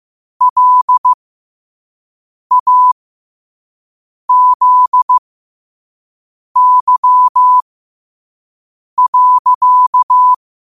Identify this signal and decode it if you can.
Morse code, heard as a keyed tone, about 15 wpm, spaced as for 5 wpm, 1000 Hz. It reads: LAZY.
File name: qotdfiles / 5wpm